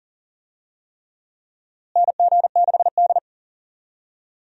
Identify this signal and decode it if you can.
Morse code, heard as a keyed tone, about 40 wpm, 700 Hz. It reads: NG6D
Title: USA Call Signs - 4 Characters - Repeated 3x - 40wpm